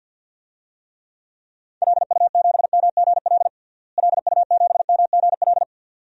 Morse code, heard as a keyed tone, about 50 wpm, 700 Hz. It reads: LU7MGL LU7MGL
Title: DXCC-100 Call Signs - 6 Characters - Repeated 2x - 50wpm